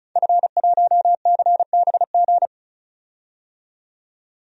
Morse code, heard as a keyed tone, about 36 wpm, 700 Hz. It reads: F1CBG
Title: DXCC-100 Call Signs - 5 Characters - 35wpm